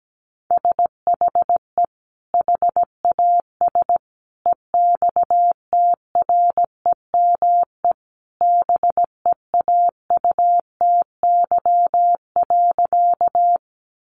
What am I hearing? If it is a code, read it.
Morse code, heard as a keyed tone, about 17 wpm, 700 Hz. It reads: SHE HAS EXTREME BEAUTY.
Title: Sentences from Top 1300 Words 17wpm